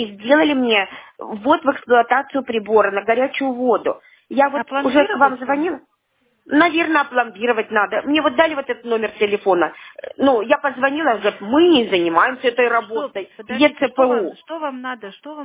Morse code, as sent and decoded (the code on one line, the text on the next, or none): none